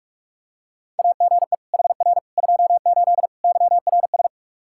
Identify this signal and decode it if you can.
Morse code, heard as a keyed tone, about 41 wpm, 700 Hz. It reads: AGE HR 28 YRS
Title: Practice QSO - Exchange 3 - More Info - 45wpm